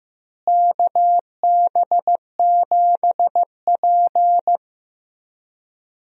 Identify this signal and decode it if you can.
Morse code, heard as a keyed tone, about 16 wpm, 700 Hz. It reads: KB7P
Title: USA Call Signs - 4 Characters - Repeated 3x - 15wpm